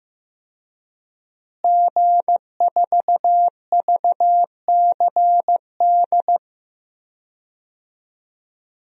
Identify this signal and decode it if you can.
Morse code, heard as a keyed tone, about 15 wpm, 700 Hz. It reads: G4VCD
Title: DXCC-100 Call Signs - 5 Characters - Repeated 2x - 15wpm